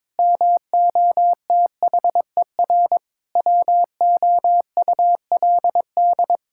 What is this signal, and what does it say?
Morse code, heard as a keyed tone, about 22 wpm, 700 Hz. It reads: MOTHER WOULD
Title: Sentences from Top 1600 Words 22wpm